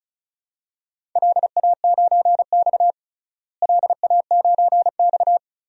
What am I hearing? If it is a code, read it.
Morse code, heard as a keyed tone, about 34 wpm, 700 Hz. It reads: LA9X LA9X
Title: DXCC-100 Call Signs - 4 Characters - Repeated 2x - 35wpm